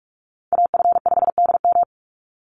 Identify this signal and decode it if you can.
Morse code, heard as a keyed tone, about 44 wpm, 700 Hz. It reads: AF5DM
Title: Call Signs - 45wpm